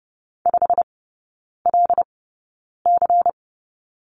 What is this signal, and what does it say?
Morse code, heard as a keyed tone, about 30 wpm, spaced as for 9 wpm, 690 Hz. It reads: 5LC
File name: Lesson 16e - Letter c - Three Characters 30wpm